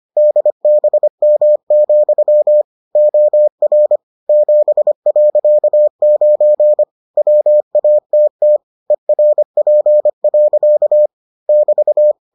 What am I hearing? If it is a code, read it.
Morse code, heard as a keyed tone, about 25 wpm, 600 Hz. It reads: DBM, OR 7.9 WATT ERP. <BT>